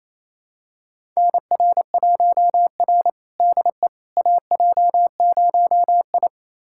Morse code, heard as a keyed tone, about 28 wpm, 700 Hz. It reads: NR1R DE AJ0I
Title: USA Callsign DE Callsign - 4 Characters - 28wpm